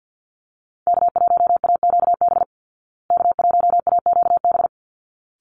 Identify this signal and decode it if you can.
Morse code, heard as a keyed tone, about 48 wpm, 700 Hz. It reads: K1AQB K1AQB
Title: Call Signs Repeated 3x - 50wpm